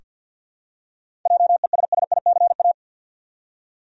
Morse code, heard as a keyed tone, about 46 wpm, 700 Hz. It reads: JESSICA
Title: USA Common Names - 50wpm